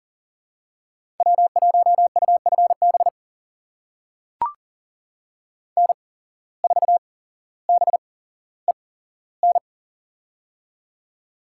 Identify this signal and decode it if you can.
Morse code, heard as a keyed tone, about 39 wpm, spaced as for 17 wpm, 700 Hz. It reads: W1UFB N4BEN
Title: USA Call Signs - 5 Characters - 8x Character Spacing - 40wpm